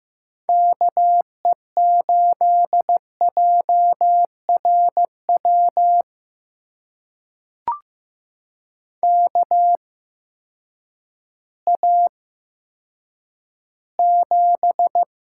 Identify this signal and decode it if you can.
Morse code, heard as a keyed tone, about 15 wpm, spaced as for 9 wpm, 700 Hz. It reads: KE8JRW KA7
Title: USA Call Signs - 6 Characters - 8x Character Spacing - 15wpm